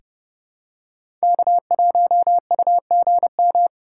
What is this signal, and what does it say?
Morse code, heard as a keyed tone, about 31 wpm, 700 Hz. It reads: K1UGM
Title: USA Call Signs - 5 Characters - 2x Character Spacing - 30wpm